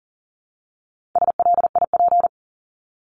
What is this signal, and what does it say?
Morse code, heard as a keyed tone, about 39 wpm, 700 Hz. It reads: SLIP
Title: Top 1751-2000 Words 40wpm